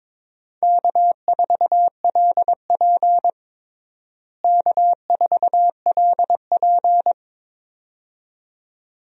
Morse code, heard as a keyed tone, about 22 wpm, 700 Hz. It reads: K4LP K4LP
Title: USA Call Signs - 4 Characters - Repeated 2x - 22wpm